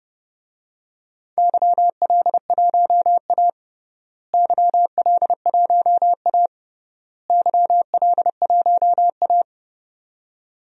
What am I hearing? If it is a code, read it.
Morse code, heard as a keyed tone, about 30 wpm, 700 Hz. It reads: YL1A YL1A YL1A
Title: DXCC-100 Call Signs - 4 Characters - Repeated 3x - 30wpm